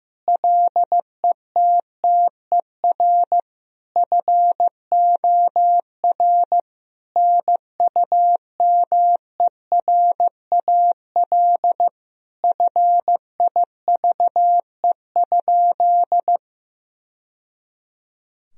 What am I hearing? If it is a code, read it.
Morse code, heard as a keyed tone, about 15 wpm, 700 Hz. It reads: LETTER FOR NUMERAL FIVE?
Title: Sentences from Top 1000 Words - Review Part 2 - 15wpm